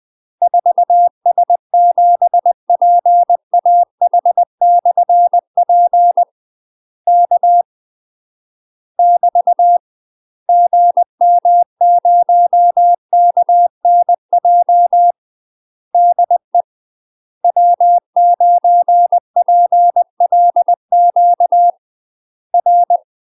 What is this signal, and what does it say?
Morse code, heard as a keyed tone, about 20 wpm, 700 Hz. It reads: 4S7PAH/P K <BT> GM0KNJ DE W9PLQ R
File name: UK MASTER QSOs TRAINING/20WPM